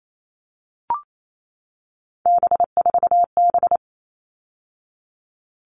Morse code, heard as a keyed tone, about 28 wpm, 700 Hz. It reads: B4B
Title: All Cut Numbers - 3-Digit 28wpm